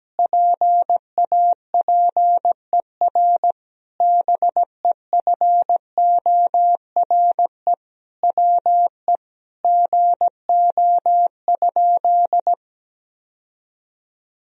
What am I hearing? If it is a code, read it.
Morse code, heard as a keyed tone, about 17 wpm, 700 Hz. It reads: PAPER BEFORE WE GO?
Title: Sentences from Top 1000 Words - Review Part 2 - 17wpm